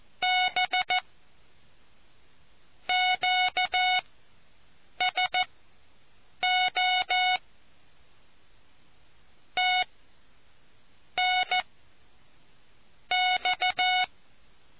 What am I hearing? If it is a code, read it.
Morse code, heard as a keyed tone, about 14 wpm, spaced as for 6 wpm, 740 Hz. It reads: B QSO TNX